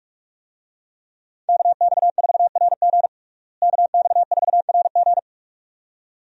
Morse code, heard as a keyed tone, about 41 wpm, 700 Hz. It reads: KX4RG KX4RG
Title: USA Call Signs - 5 Characters - Repeated 2x - 45wpm